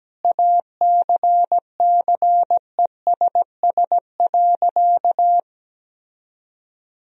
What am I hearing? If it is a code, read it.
Morse code, heard as a keyed tone, about 17 wpm, 700 Hz. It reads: ACCESS.
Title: Sentences from Top 1000 Words - Review Part 4 - 17wpm